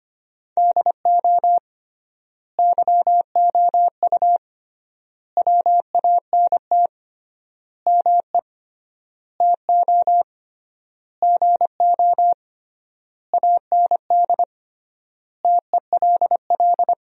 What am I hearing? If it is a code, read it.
Morse code, heard as a keyed tone, about 25 wpm, 700 Hz. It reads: DO YOU WANT ME TO GO AND TELL
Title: 3x Word-Spacing - Sentences from Top 200 Words - 25wpm